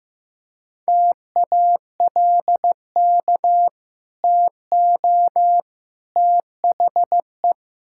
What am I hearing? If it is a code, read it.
Morse code, heard as a keyed tone, about 15 wpm, 700 Hz. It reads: TALK TO THE
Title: Sentences from Top 1600 Words 15wpm